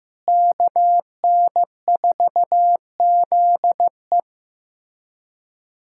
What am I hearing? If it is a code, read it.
Morse code, heard as a keyed tone, about 16 wpm, 700 Hz. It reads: KN4ZE
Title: USA Callsign DE Callsign - 5 Characters - 15wpm